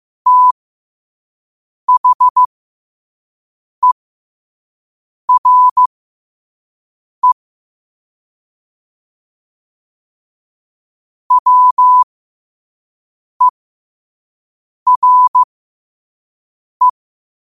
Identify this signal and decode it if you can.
Morse code, heard as a keyed tone, about 15 wpm, spaced as for 5 wpm, 1000 Hz. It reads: THERE WERE
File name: qotdfiles / 5wpm